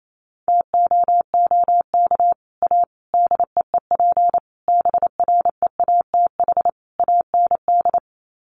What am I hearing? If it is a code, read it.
Morse code, heard as a keyed tone, about 28 wpm, 700 Hz. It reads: TOOK A DEEP BREATH AND
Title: Sets of 5 Words Encore 28wpm